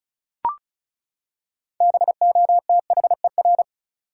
Morse code, heard as a keyed tone, about 36 wpm, 700 Hz. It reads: BOTHER
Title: Six Letter Words - 35wpm